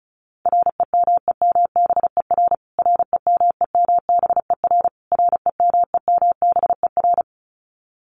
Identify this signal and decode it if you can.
Morse code, heard as a keyed tone, about 34 wpm, 700 Hz. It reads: REMEMBER REMEMBER REMEMBER
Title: Top 500 Words Repeated 35wpm